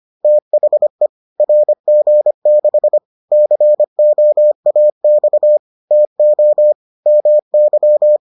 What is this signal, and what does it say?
Morse code, heard as a keyed tone, about 25 wpm, 600 Hz. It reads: THE RG6 COAX TO MY